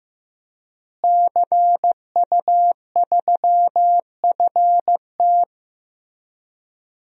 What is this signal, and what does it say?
Morse code, heard as a keyed tone, about 15 wpm, 700 Hz. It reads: CU3FT
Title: DXCC-100 Call Signs - 5 Characters - Repeated 3x - 15wpm